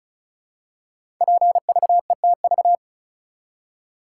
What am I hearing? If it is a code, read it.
Morse code, heard as a keyed tone, about 36 wpm, 700 Hz. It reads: PVETV
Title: Code Groups - 5 Letters - 35wpm